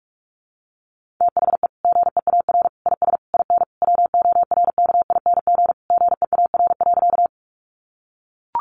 Morse code, heard as a keyed tone, about 44 wpm, 700 Hz. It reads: THE GEAR IS IN WORKING GEAR.
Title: Sentences from Top 1400 Words 45wpm